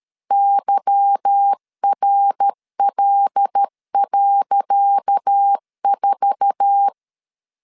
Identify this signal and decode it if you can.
Morse code, heard as a keyed tone, about 13 wpm, 800 Hz. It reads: YRL.4
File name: LEZIONI / 10 SIGNS / 10C DOT 12wpm 2451